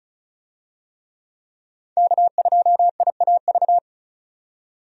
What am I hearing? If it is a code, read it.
Morse code, heard as a keyed tone, about 34 wpm, 700 Hz. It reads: K2IAV
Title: USA Call Signs - 5 Characters - 35wpm